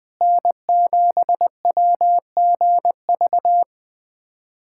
Morse code, heard as a keyed tone, about 20 wpm, 700 Hz. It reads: N7WGV
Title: USA Callsign DE Callsign - 5 Characters - 20wpm